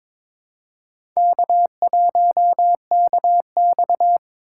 Morse code, heard as a keyed tone, about 22 wpm, 700 Hz. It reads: K1KX
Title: USA Call Signs - 4 Characters - 2x Character Spacing - 22wpm